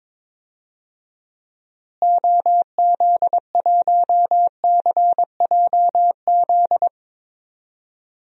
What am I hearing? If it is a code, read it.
Morse code, heard as a keyed tone, about 22 wpm, 700 Hz. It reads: OZ1CJZ
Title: DXCC-100 Call Signs - 6 Characters - Repeated 3x - 22wpm